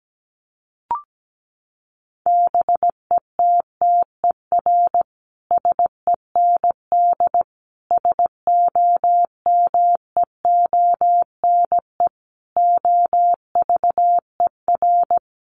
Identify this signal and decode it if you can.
Morse code, heard as a keyed tone, about 17 wpm, 700 Hz. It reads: BETTER SEND SOMEONE OVER
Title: Sentences from Top 1500 Words 17wpm